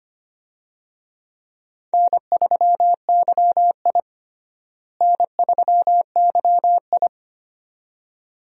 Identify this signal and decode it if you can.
Morse code, heard as a keyed tone, about 25 wpm, 700 Hz. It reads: N3YI N3YI
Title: USA Call Signs - 4 Characters - Repeated 2x - 25wpm